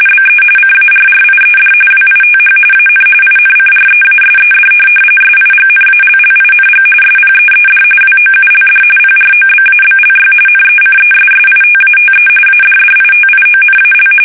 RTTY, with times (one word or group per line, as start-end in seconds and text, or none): 0.1-0.4 BE
0.6-1.7 CARRIED
1.9-2.6 WITH
2.7-3.0 AN
3.2-4.7 INCREASED
5.2-6.8 SIGNALLING
7.0-7.8 SPEED
8.0-8.8 WOULD
9.0-10.3 PROBABLY
10.5-12.3 NECESSITATE
12.5-14.1 ADDITIONAL